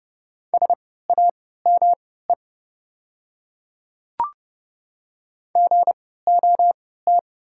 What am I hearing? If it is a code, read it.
Morse code, heard as a keyed tone, about 29 wpm, spaced as for 17 wpm, 690 Hz. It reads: SAME GOT
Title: 3x Character Spacing - Top 300 Words - 30wpm